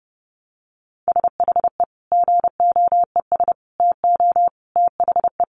Morse code, heard as a keyed tone, about 30 wpm, 700 Hz. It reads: SHE GOES TO THE
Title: Easy Phrases 30wpm